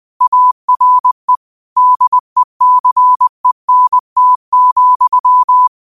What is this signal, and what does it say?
Morse code, heard as a keyed tone, about 20 wpm, 1000 Hz. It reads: ARE DECENT,